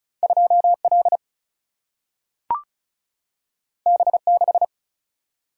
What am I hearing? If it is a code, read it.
Morse code, heard as a keyed tone, about 36 wpm, 700 Hz. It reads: 2 L B 6
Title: Single-Letter Single-Number 35wpm